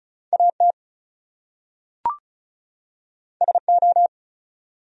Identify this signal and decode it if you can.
Morse code, heard as a keyed tone, about 36 wpm, 700 Hz. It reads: AT SO
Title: Top 100 Words 35wpm